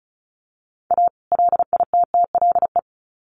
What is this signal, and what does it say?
Morse code, heard as a keyed tone, about 34 wpm, 700 Hz. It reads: A LITTLE
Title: Sets of 2 Words 35wpm